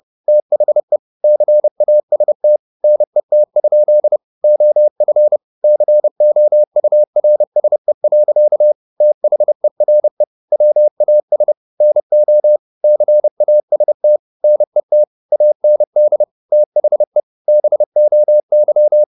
Morse code, heard as a keyed tone, about 30 wpm, 600 Hz. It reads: THE CAST NET? OF COURSE. THERE WAS NO CAST NET AND THE BOY
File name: the old man and the sea / the old man and the sea 1 30wpm